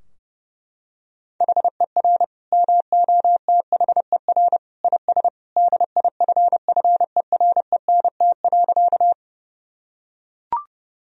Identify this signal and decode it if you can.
Morse code, heard as a keyed tone, about 30 wpm, 700 Hz. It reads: HER MOTHER IS DIFFERENT.
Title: Sentences from Top 1000 Words - Review Part 3 - 30wpm